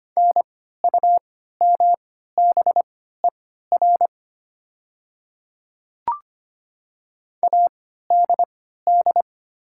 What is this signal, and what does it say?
Morse code, heard as a keyed tone, about 25 wpm, spaced as for 14 wpm, 700 Hz. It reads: NUMBER ADD
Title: 3x Character Spacing - Top 200 Words - 25wpm